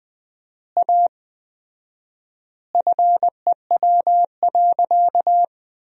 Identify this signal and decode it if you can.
Morse code, heard as a keyed tone, about 20 wpm, 700 Hz. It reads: A FEW.
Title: 4x Word-Spacing - Sentences from Top 300 Words - 20wpm